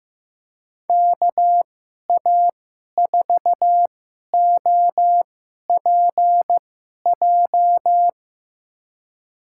Morse code, heard as a keyed tone, about 15 wpm, spaced as for 11 wpm, 700 Hz. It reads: KA4OPJ
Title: USA Call Signs - 6 Characters - 2x Character Spacing - 15wpm